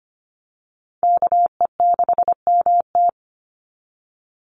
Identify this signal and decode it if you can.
Morse code, heard as a keyed tone, about 25 wpm, 700 Hz. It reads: KE6MT
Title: NA SOTA Chaser Callsigns 25wpm